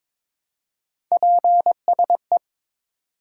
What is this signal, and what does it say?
Morse code, heard as a keyed tone, about 21 wpm, 700 Hz. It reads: PSE
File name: Top 90 QSO Elements - 2x Character Spacing - 22wpm